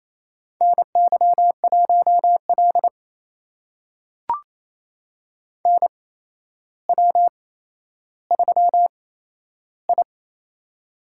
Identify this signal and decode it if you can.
Morse code, heard as a keyed tone, about 28 wpm, spaced as for 12 wpm, 700 Hz. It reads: NY1L NW3I
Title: USA Call Signs - 4 Characters - 8x Character Spacing - 28wpm